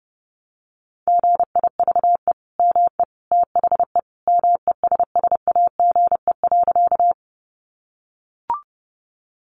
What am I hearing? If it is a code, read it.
Morse code, heard as a keyed tone, about 30 wpm, 700 Hz. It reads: GIVE ME THE MESSAGE.
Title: Sentences from Top 1100 Words 30wpm